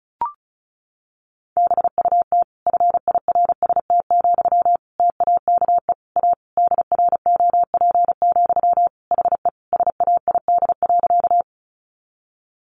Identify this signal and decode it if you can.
Morse code, heard as a keyed tone, about 34 wpm, 700 Hz. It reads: BUT FIRST, TAKE A DROP, HE SAID.
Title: Sentences from Top 1400 Words 35wpm